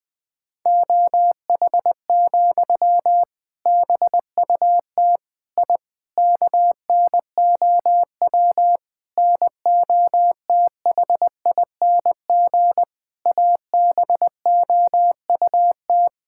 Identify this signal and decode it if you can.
Morse code, heard as a keyed tone, about 20 wpm, 700 Hz. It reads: OH, BUT I KNOW NOTHING ABOUT
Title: Sentences from Top 1000 Words - Review Part 6 - 20wpm